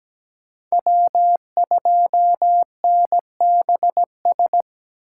Morse code, heard as a keyed tone, about 17 wpm, 700 Hz. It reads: W2NBS
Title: USA Call Signs - 5 Characters - 4x Character Spacing - 17wpm